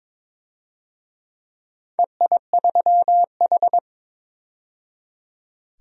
Morse code, heard as a keyed tone, about 22 wpm, 690 Hz. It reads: EI3H
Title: DXCC-100 Call Signs - 4 Characters - 22wpm